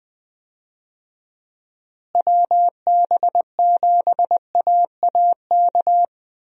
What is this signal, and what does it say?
Morse code, heard as a keyed tone, about 21 wpm, 700 Hz. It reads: WB7AAK
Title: USA Call Signs - 6 Characters - Repeated 2x - 20wpm